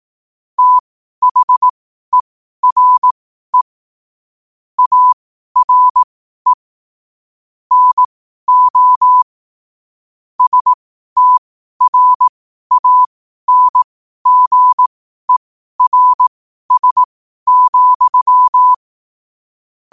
Morse code, heard as a keyed tone, about 18 wpm, spaced as for 12 wpm, 1000 Hz. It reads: THERE ARE NO STRANGERS,
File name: qotdfiles / 12wpm